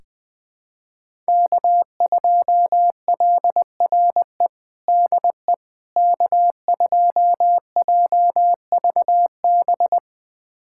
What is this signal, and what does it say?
Morse code, heard as a keyed tone, about 21 wpm, 700 Hz. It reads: K2LRE DE K2JVB
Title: USA Callsign DE Callsign - 5 Characters - 20wpm